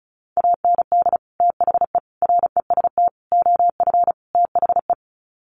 Morse code, heard as a keyed tone, about 34 wpm, 700 Hz. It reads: AND THE REST OF THE
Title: Sets of 5 Words 35wpm